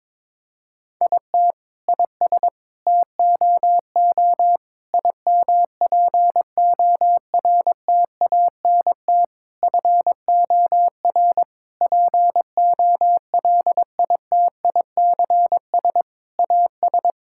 Morse code, heard as a keyed tone, about 22 wpm, 700 Hz. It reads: IT IS TOO IMPORTANT FOR POLITICS AS
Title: Sentences from Top 1000 Words - Review Part 6 - 22wpm